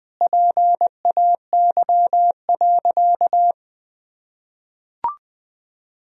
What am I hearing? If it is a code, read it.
Morse code, heard as a keyed tone, about 20 wpm, 700 Hz. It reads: PAY.
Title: Sentences from Top 1000 Words - Review Part 4 - 20wpm